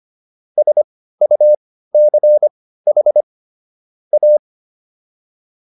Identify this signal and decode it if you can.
Morse code, heard as a keyed tone, about 25 wpm, spaced as for 15 wpm, 600 Hz. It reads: SUCH A